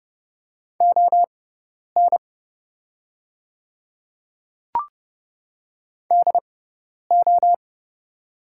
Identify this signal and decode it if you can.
Morse code, heard as a keyed tone, about 30 wpm, spaced as for 10 wpm, 700 Hz. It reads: ON DO
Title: 6x Character Spacing - Top 200 Words - 30wpm